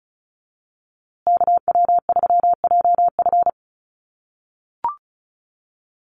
Morse code, heard as a keyed tone, about 34 wpm, 700 Hz. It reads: KW3JF
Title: USA Call Signs Repeated 3x 35wpm